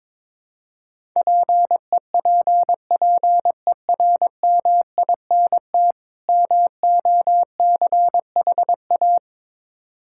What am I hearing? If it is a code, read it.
Morse code, heard as a keyed tone, about 22 wpm, 700 Hz. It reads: PEPPERMINT MOCHA
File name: Difficult Words with Hints - 22wpm